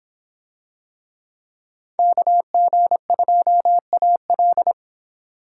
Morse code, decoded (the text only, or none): KG2AL